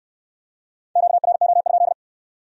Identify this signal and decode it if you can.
Morse code, heard as a keyed tone, about 65 wpm, 700 Hz. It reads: QRO?